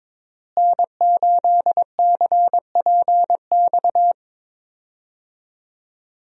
Morse code, heard as a keyed tone, about 22 wpm, 700 Hz. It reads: N8CPX
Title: USA Call Signs - 5 Characters - Repeated 3x - 22wpm